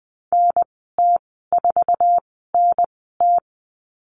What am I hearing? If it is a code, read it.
Morse code, heard as a keyed tone, about 20 wpm, spaced as for 15 wpm, 700 Hz. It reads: NT4NT
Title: 048 Lesson 9g - Number 4 - Call Signs 20wpm